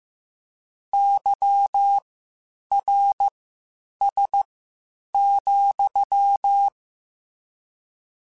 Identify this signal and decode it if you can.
Morse code, heard as a keyed tone, about 15 wpm, spaced as for 8 wpm, 800 Hz. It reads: YRS,
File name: (spaced) / QSOs Spaced-mp3s / WD0BGP 1 @13 sp